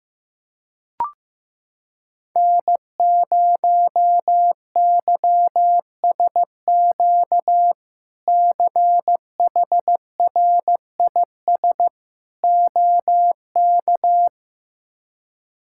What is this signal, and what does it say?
Morse code, heard as a keyed tone, about 16 wpm, 700 Hz. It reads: N0YSQ CHRIS OK
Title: K1USN SST QSO Practice - 15wpm